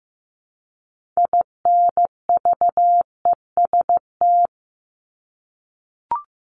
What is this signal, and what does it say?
Morse code, heard as a keyed tone, about 15 wpm, 700 Hz. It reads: INVEST
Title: Top 1001-1250 Words 15wpm